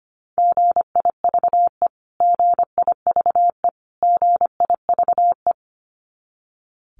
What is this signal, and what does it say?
Morse code, heard as a keyed tone, about 25 wpm, 700 Hz. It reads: GIVE GIVE GIVE
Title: Top 200 Words Repeated 25wpm